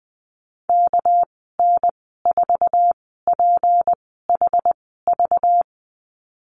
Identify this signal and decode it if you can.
Morse code, heard as a keyed tone, about 20 wpm, spaced as for 14 wpm, 700 Hz. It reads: KN4PHV
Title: 238 Lesson 38g - Letter q - Call Signs 20wpm